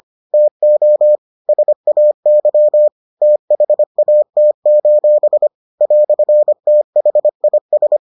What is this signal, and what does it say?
Morse code, heard as a keyed tone, about 25 wpm, 600 Hz. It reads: TO SAY THAT: "THIS